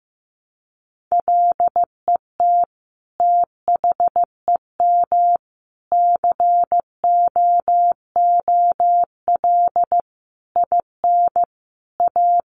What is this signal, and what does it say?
Morse code, heard as a keyed tone, about 15 wpm, 700 Hz. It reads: LET THEM COOL IN A
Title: Sentences from Top 1200 Words 15wpm